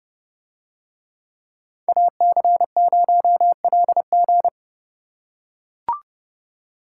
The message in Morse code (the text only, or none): AC0LG